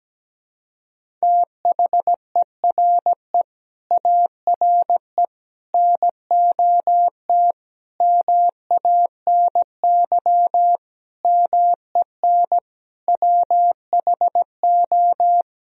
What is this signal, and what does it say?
Morse code, heard as a keyed tone, about 17 wpm, 700 Hz. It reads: THERE ARE NOT MANY MEN WHO
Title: Sentences from Top 1000 Words - Review Part 1 - 17wpm